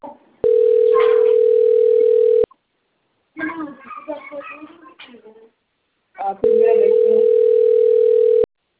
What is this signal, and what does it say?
Ringback tone (North America), 440 + 480 Hz, with speech.
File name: out- - - - - .69097